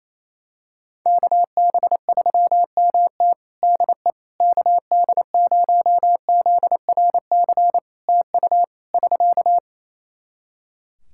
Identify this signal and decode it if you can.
Morse code, heard as a keyed tone, about 27 wpm, 700 Hz. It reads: KB3MT DE KD0ZRC TU <SK>
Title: Practice QSO - Exchange 4 - Ending - 28wpm